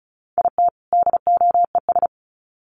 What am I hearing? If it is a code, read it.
Morse code, heard as a keyed tone, about 34 wpm, 700 Hz. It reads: IT DOES
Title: Sets of 2 Words 35wpm